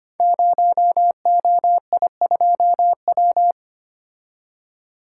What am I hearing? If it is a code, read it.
Morse code, heard as a keyed tone, about 25 wpm, 700 Hz. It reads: 0OI2W
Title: Code Groups - 5 Letter:Numbers - 25wpm